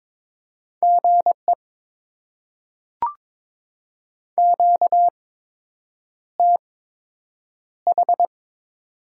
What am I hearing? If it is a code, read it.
Morse code, heard as a keyed tone, about 21 wpm, spaced as for 8 wpm, 700 Hz. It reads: GE QTH
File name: Top 20 QSO Elements - 8x Character Spacing - 22wpm